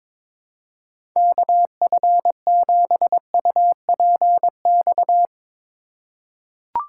Morse code, heard as a keyed tone, about 21 wpm, 700 Hz. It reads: KF7UPX
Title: USA Call Signs - 6 Characters - 8x Character Spacing - 22wpm